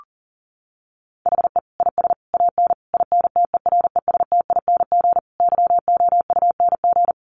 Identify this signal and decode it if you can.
Morse code, heard as a keyed tone, about 40 wpm, 700 Hz. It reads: HE IS AN INTERESTING YOUNG
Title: Sentences from Top 1000 Words 40wpm